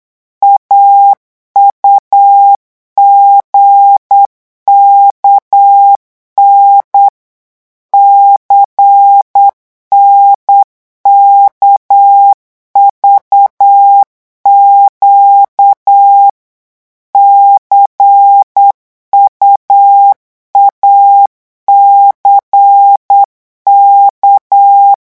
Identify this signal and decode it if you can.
Morse code, heard as a keyed tone, about 8 wpm, 800 Hz. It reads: AUGKN CNKVQ CUACK
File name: morse mp3 / revision3-vitesse 8 mots